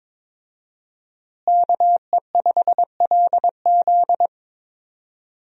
Morse code, heard as a keyed tone, about 22 wpm, 700 Hz. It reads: KE5LZ
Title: USA Call Signs - 5 Characters - Repeated 3x - 22wpm